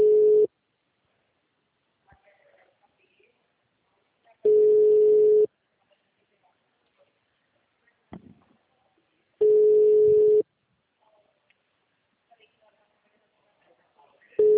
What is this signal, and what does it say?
Ringback tone (Europe, 425 Hz).
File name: out- -2011- - - .599